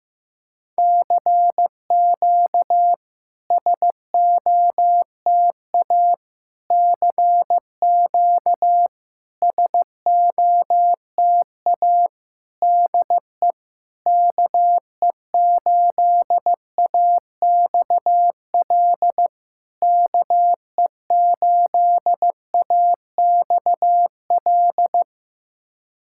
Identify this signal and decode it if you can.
Morse code, heard as a keyed tone, about 15 wpm, 690 Hz. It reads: CQ SOTA CQ SOTA DE KE8AXL KE8AXL
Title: SOTA QSO Practice 15wpm